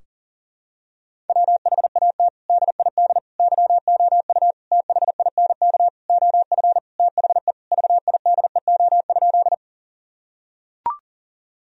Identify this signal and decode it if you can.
Morse code, heard as a keyed tone, about 40 wpm, 700 Hz. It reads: WHAT DID YOU THINK OF THE VIDEO?
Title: Sentences from Top 1000 Words - Review Part 4 - 40wpm